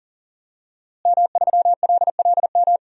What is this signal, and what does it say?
Morse code, heard as a keyed tone, about 40 wpm, 700 Hz. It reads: M3LLM
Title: DXCC-100 Call Signs - 5 Characters - Repeated 2x - 40wpm